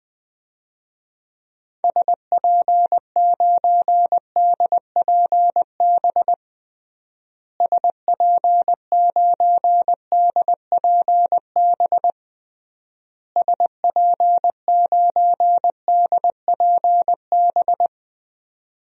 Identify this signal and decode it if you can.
Morse code, heard as a keyed tone, about 20 wpm, 700 Hz. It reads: SP9DPB SP9DPB SP9DPB
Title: DXCC-100 Call Signs - 6 Characters - Repeated 3x - 20wpm